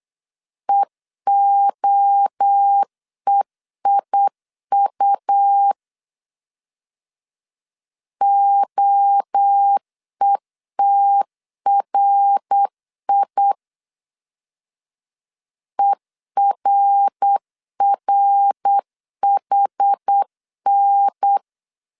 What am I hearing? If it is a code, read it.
Morse code, heard as a keyed tone, about 8 wpm, 800 Hz. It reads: EOEIU OETRI ERRHN